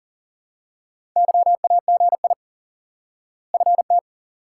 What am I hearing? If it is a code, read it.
Morse code, heard as a keyed tone, about 40 wpm, 700 Hz. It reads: YAGI FT
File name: IQE Course 3 - Lesson 89 - YAGI - Two Elements - 40wpm